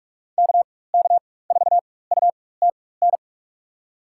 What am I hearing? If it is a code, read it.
Morse code, heard as a keyed tone, about 42 wpm, spaced as for 21 wpm, 700 Hz. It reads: KK4UTN
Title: USA Call Signs - 4x Character Spacing - 45wpm